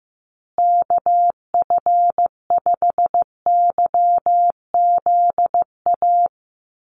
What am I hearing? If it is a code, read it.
Morse code, heard as a keyed tone, about 15 wpm, 700 Hz. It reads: KF5YZA
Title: USA Call Signs Repeated 3x 15wpm